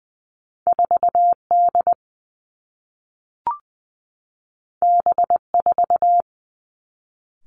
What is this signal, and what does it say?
Morse code, heard as a keyed tone, about 20 wpm, 700 Hz. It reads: 4 D B 4
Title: Two Challenging Letter-Number Combinations 20wpm